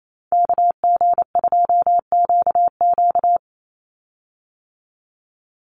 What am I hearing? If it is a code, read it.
Morse code, heard as a keyed tone, about 28 wpm, 700 Hz. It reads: KG2QQ
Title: USA Call Signs 28wpm